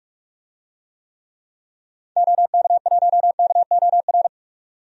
Morse code, heard as a keyed tone, about 42 wpm, 700 Hz. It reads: OK1KOR
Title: DXCC-100 Call Signs - 6 Characters - 45wpm